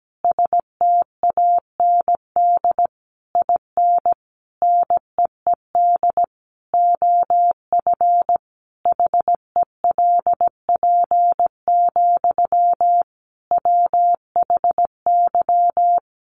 Morse code, heard as a keyed tone, about 17 wpm, 700 Hz. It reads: STAND IN NEED OF HELP, WHY
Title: Sentences from Top 1200 Words 17wpm